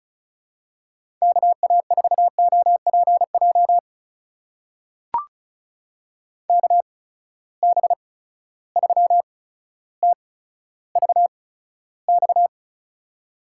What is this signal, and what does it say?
Morse code, heard as a keyed tone, about 34 wpm, spaced as for 15 wpm, 700 Hz. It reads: KA4OPJ KB3TVX
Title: USA Call Signs - 6 Characters - 8x Character Spacing - 35wpm